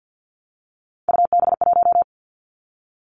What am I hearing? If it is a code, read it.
Morse code, heard as a keyed tone, about 50 wpm, 700 Hz. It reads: UB1
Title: Three Challenging Letter-Number Combinations 50wpm